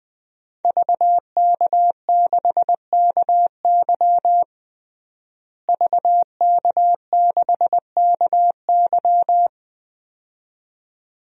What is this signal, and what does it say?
Morse code, heard as a keyed tone, about 20 wpm, 700 Hz. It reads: VK6KY VK6KY
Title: DXCC-100 Call Signs - 5 Characters - Repeated 2x - 20wpm